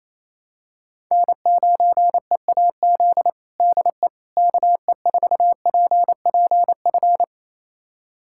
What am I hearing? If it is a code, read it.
Morse code, heard as a keyed tone, about 27 wpm, 700 Hz. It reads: N9EAZ DE KE4PPF
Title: USA Callsign DE Callsign 28wpm